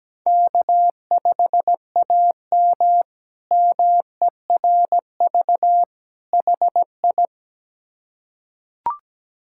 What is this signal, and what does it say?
Morse code, heard as a keyed tone, about 17 wpm, 700 Hz. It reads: K5AM MERV HI
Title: K1USN SST QSO Practice - 17wpm